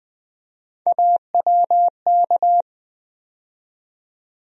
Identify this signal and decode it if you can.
Morse code, heard as a keyed tone, about 21 wpm, 700 Hz. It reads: AWK
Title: USA Call Sign Postfixes - 3 Letters 20wpm